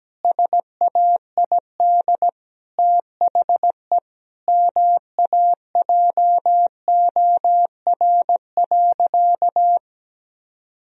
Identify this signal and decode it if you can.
Morse code, heard as a keyed tone, about 17 wpm, 700 Hz. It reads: SAID THE MAJOR.
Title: Sentences from Top 1000 Words - Review Part 4 - 17wpm